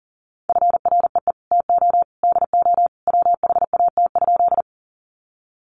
Morse code, heard as a keyed tone, about 40 wpm, 700 Hz. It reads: FREE TO DO WHAT?
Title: Sentences from Top 500 Words 40wpm